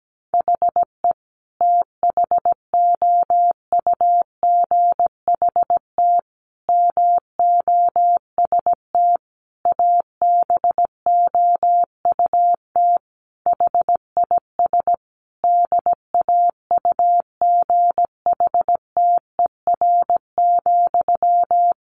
Morse code, heard as a keyed tone, about 17 wpm, 700 Hz. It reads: HE THOUGHT MOST ABOUT HIS DAUGHTER,
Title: Sentences from Top 1500 Words 17wpm